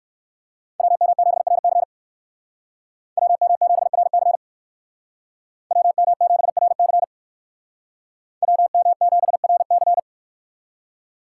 Morse code, heard as a keyed tone, about 56 wpm, 700 Hz. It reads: WM7RC WM7RC WM7RC WM7RC
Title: Speed Racing - USA 2x2 Call Signs 45wpm